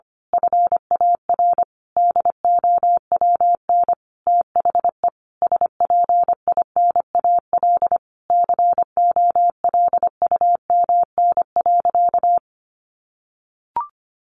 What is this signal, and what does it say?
Morse code, heard as a keyed tone, about 25 wpm, 700 Hz. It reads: FAR DOWN THE SPINAL COLUMN.
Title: A Sound Of Thunder 25wpm